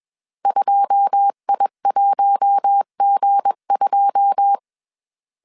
Morse code, heard as a keyed tone, about 21 wpm, 800 Hz. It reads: 2I1G2
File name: CW / DET / 1 9 1500C S C 20wpm 2310